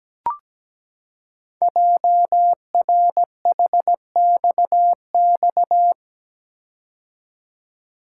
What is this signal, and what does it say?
Morse code, heard as a keyed tone, about 17 wpm, 700 Hz. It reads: JRHXX
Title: Code Groups - 5 Letters - 17wpm